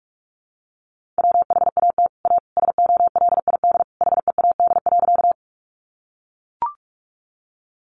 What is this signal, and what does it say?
Morse code, heard as a keyed tone, about 44 wpm, 700 Hz. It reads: WHAT A SOLID HEAD.
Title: Sentences from Top 1500 Words 45wpm